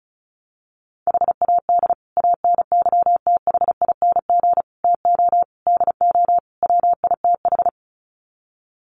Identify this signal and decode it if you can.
Morse code, heard as a keyed tone, about 34 wpm, 700 Hz. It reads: HAD ANYTHING TO DO WITH
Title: Sets of 5 Words Encore 35wpm